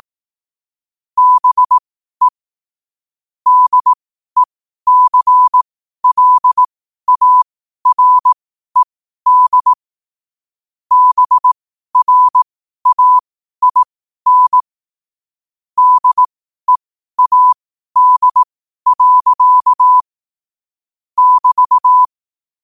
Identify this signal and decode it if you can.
Morse code, heard as a keyed tone, about 18 wpm, spaced as for 12 wpm, 1000 Hz. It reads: BE DECLARED BRAIN DEAD. <BT>
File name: qotdfiles / 12wpm